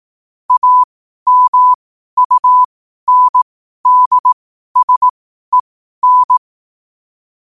Morse code, heard as a keyed tone, about 18 wpm, spaced as for 12 wpm, 1000 Hz. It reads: AMUNDSEN
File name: qotdfiles / 12wpm